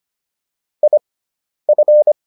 Morse code, heard as a keyed tone, about 25 wpm, spaced as for 10 wpm, 600 Hz. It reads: IF